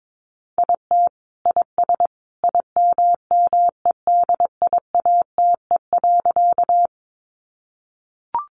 Morse code, heard as a keyed tone, about 22 wpm, 700 Hz. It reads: IT IS IMMEDIATE.
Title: Sentences from Top 1200 Words 22wpm